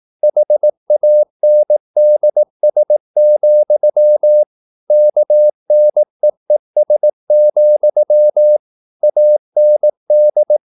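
Morse code, heard as a keyed tone, about 18 wpm, 600 Hz. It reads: HANDS, KNEES, AND
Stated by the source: Morse practice at 18 wpm, English text, from CW med Gnister.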